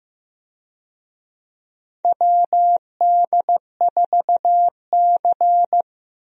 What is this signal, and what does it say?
Morse code, heard as a keyed tone, about 15 wpm, 700 Hz. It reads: WD4C
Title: USA Call Signs - 4 Characters - 15wpm